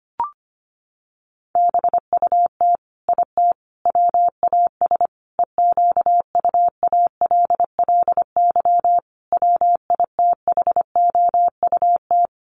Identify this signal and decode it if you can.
Morse code, heard as a keyed tone, about 25 wpm, 700 Hz. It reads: BUT IT WAS EQUALLY WITHOUT
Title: Sentences from Top 1400 Words 25wpm